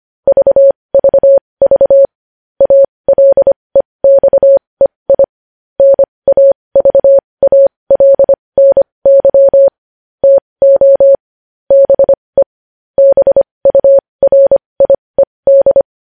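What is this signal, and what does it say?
Morse code, heard as a keyed tone, about 25 wpm, spaced as for 20 wpm, 550 Hz. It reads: VVV ALEXEI NAVALNY TO BE BURIED
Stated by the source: Morse code news headlines sent at 20 wpm.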